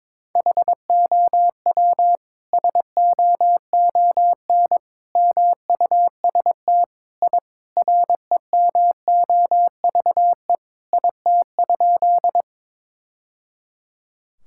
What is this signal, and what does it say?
Morse code, heard as a keyed tone, about 22 wpm, 700 Hz. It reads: HOW SOON MUST I REMOVE IT?
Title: Sentences from Top 1000 Words - Review Part 5 - 22wpm